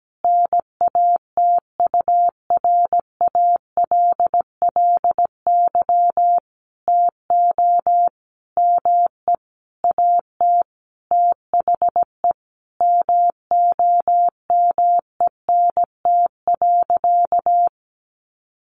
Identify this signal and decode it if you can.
Morse code, heard as a keyed tone, about 17 wpm, 700 Hz. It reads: NATURALLY TO ME AT THE MOMENT.
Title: Sentences from Top 1300 Words 17wpm